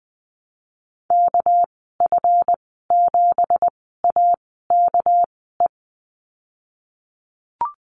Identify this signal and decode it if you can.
Morse code, heard as a keyed tone, about 19 wpm, spaced as for 14 wpm, 700 Hz. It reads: KF7AKE
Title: 207 Lesson 33g - Letter j - Call Signs 20wpm